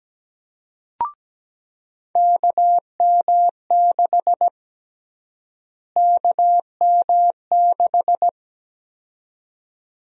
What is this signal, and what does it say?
Morse code, heard as a keyed tone, about 17 wpm, 700 Hz. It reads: KM6 KM6
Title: USA Call Sign Prefixes - Repeated 2x - 17wpm